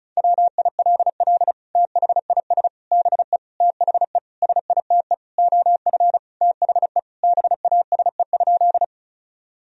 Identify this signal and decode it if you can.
Morse code, heard as a keyed tone, about 34 wpm, 700 Hz. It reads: WILL THIS BE THE SITE OF THE BASE?
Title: Sentences from Top 1000 Words - Review Part 1 - 35wpm